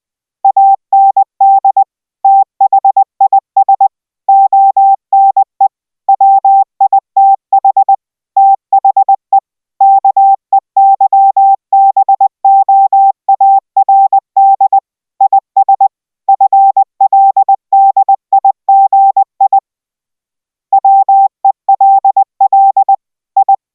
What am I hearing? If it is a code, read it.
Morse code, heard as a keyed tone, about 20 wpm, 780 Hz. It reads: AND THIS ONE WITH THE KEYBOARD IS FLDIGI WELL I